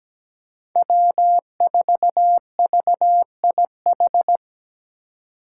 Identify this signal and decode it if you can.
Morse code, heard as a keyed tone, about 17 wpm, 700 Hz. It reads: W4VIH
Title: Code Groups - 5 Letter:Numbers - 17wpm